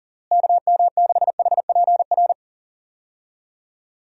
Morse code, heard as a keyed tone, about 40 wpm, 700 Hz. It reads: KM6HPR
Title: USA Call Signs - 6 Characters - Repeated 3x - 40wpm